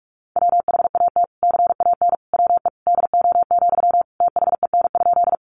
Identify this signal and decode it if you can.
Morse code, heard as a keyed tone, about 44 wpm, 700 Hz. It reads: WHAT CAN WE DO, THEN?
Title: Sentences from Top 100 Words 45wpm